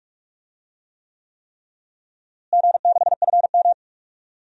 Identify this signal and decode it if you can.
Morse code, heard as a keyed tone, about 46 wpm, 700 Hz. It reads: G6FM
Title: DXCC-100 Call Signs - 4 Characters - Repeated 3x - 45wpm